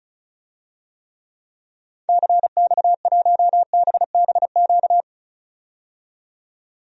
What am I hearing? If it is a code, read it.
Morse code, heard as a keyed tone, about 34 wpm, 700 Hz. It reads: CX1BBQ
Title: DXCC-100 Call Signs - 6 Characters - 35wpm